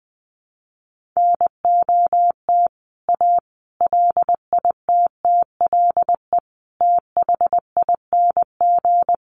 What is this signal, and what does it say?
Morse code, heard as a keyed tone, about 20 wpm, 700 Hz. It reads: NOT A LITTLE THING
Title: A Sound Of Thunder 20wpm